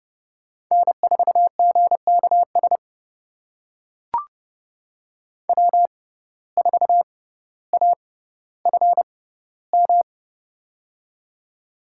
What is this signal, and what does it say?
Morse code, heard as a keyed tone, about 30 wpm, spaced as for 15 wpm, 700 Hz. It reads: N4GKS W4AFM
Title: USA Call Signs - 5 Characters - 6x Character Spacing - 30wpm